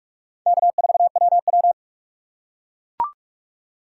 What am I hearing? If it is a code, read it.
Morse code, heard as a keyed tone, about 43 wpm, 700 Hz. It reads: K4WW